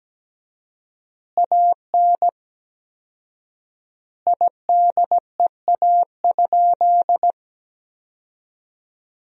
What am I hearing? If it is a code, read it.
Morse code, heard as a keyed tone, about 17 wpm, 700 Hz. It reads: AN IDEA?
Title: 4x Word-Spacing - Sentences from Top 400 Words - 17wpm